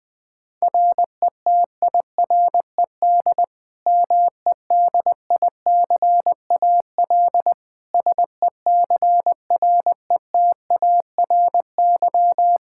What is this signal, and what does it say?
Morse code, heard as a keyed tone, about 21 wpm, 700 Hz. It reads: RETIRED MEDICAL SECRETARY
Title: K1USN SST QSO Practice 20wpm